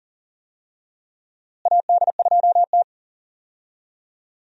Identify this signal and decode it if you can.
Morse code, heard as a keyed tone, about 40 wpm, 700 Hz. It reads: AD2T
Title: USA Call Signs - 4 Characters - Repeated 3x - 40wpm